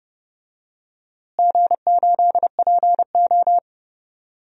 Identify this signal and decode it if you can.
Morse code, heard as a keyed tone, about 32 wpm, 700 Hz. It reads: G8PO
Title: DXCC-100 Call Signs - 4 Characters - Repeated 3x - 30wpm